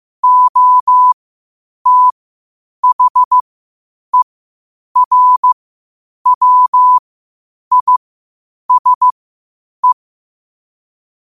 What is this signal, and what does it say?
Morse code, heard as a keyed tone, about 15 wpm, spaced as for 8 wpm, 1000 Hz. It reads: OTHERWISE
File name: qotdfiles / 8wpm